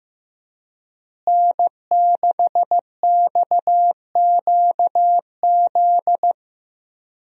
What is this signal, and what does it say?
Morse code, heard as a keyed tone, about 15 wpm, 700 Hz. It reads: N6XQZ
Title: USA Call Signs - 5 Characters - Repeated 2x - 15wpm